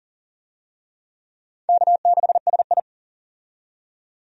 Morse code, heard as a keyed tone, about 40 wpm, 700 Hz. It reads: K6SI
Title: USA Call Signs - 4 Characters - 40wpm